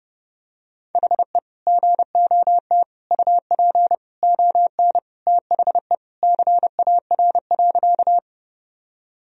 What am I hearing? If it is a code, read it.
Morse code, heard as a keyed tone, about 30 wpm, 700 Hz. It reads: HE GOT UP ON THE CAR.
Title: Sentences from Top 1000 Words - Review Part 1 - 30wpm